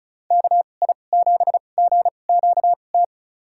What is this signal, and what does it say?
Morse code, heard as a keyed tone, about 34 wpm, spaced as for 25 wpm, 700 Hz. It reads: KI7GQT
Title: USA Call Signs - 6 Characters - 2x Character Spacing - 35wpm